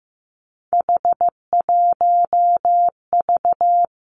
Morse code, heard as a keyed tone, about 15 wpm, 700 Hz. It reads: H1V
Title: Three Challenging Letter-Number Combinations 15wpm